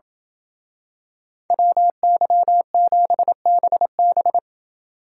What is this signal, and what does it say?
Morse code, heard as a keyed tone, about 28 wpm, 700 Hz. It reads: WY7BB